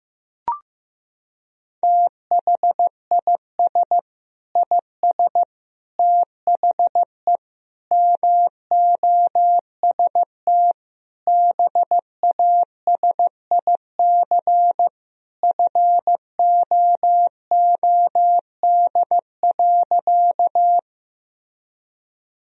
Morse code, heard as a keyed tone, about 15 wpm, 700 Hz. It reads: THIS IS THE MOST BASIC FOOD.
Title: Sentences from Top 1000 Words - Review Part 3 - 15wpm